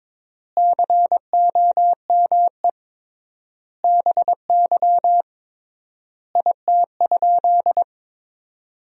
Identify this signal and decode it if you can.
Morse code, heard as a keyed tone, about 22 wpm, 700 Hz. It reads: COME BY IT?
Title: 3x Word-Spacing - Sentences from Top 100 Words - 22wpm